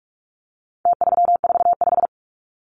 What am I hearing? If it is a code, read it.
Morse code, heard as a keyed tone, about 44 wpm, 700 Hz. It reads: T345
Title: Numbers - Contest Style - 4-Digit 45wpm